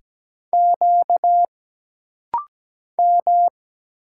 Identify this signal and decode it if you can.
Morse code, heard as a keyed tone, about 17 wpm, 700 Hz. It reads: Q M
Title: Single Letters - Rapid-Fire 17wpm